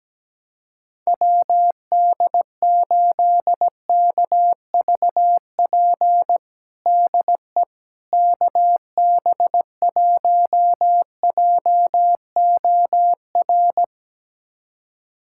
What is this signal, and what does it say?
Morse code, heard as a keyed tone, about 17 wpm, 700 Hz. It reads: WD8KVP DE KB1JOR
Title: USA Callsign DE Callsign - 6 Characters - 17wpm